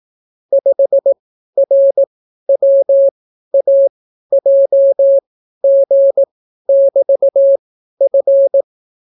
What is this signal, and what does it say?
Morse code, heard as a keyed tone, about 18 wpm, spaced as for 12 wpm, 550 Hz. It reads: 5RWAJG<BT>F